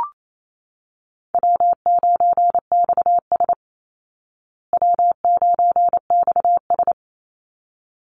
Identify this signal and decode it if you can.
Morse code, heard as a keyed tone, about 28 wpm, 690 Hz. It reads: W9XS W9XS
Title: Call Signs Repeated 3x - 28wpm